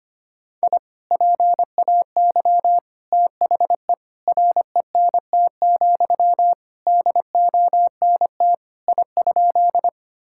Morse code, heard as a keyed tone, about 25 wpm, 700 Hz. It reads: I PAY THE RENT, DONT I?
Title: Sentences from Top 1000 Words - Review Part 5 - 25wpm